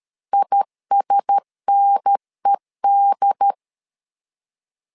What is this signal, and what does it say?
Morse code, heard as a keyed tone, about 13 wpm, 800 Hz. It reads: ISNED
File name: LEZIONI / 4 Q1S 12wpm 1802